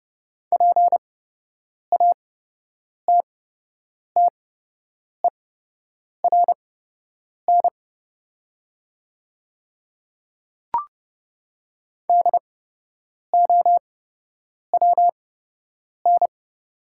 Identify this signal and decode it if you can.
Morse code, heard as a keyed tone, about 30 wpm, spaced as for 8 wpm, 700 Hz. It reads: PATTERN DOWN
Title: 8x Character Spacing - Top 500 Words - 30wpm